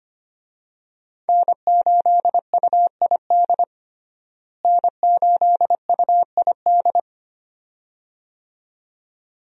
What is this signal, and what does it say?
Morse code, heard as a keyed tone, about 25 wpm, 700 Hz. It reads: N8UID N8UID
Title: USA Call Signs - 5 Characters - Repeated 2x - 25wpm